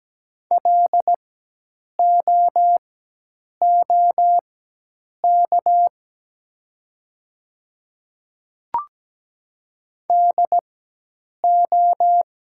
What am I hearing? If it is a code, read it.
Morse code, heard as a keyed tone, about 17 wpm, spaced as for 8 wpm, 700 Hz. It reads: LOOK DO
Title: 4x Character Spacing - Top 100 Words - 17wpm